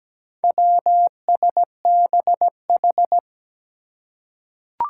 Morse code, heard as a keyed tone, about 17 wpm, 700 Hz. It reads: WSBH
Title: Four Challenging Letter-Number Combinations 17wpm